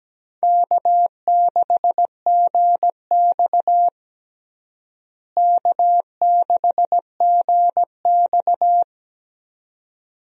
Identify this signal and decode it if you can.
Morse code, heard as a keyed tone, about 17 wpm, 700 Hz. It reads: K6GX K6GX
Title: USA Call Signs - 4 Characters - Repeated 2x - 17wpm